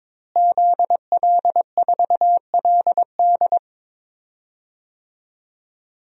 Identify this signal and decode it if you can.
Morse code, heard as a keyed tone, about 21 wpm, 700 Hz. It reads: ZL4LD
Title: DXCC-100 Call Signs - 5 Characters - Repeated 3x - 22wpm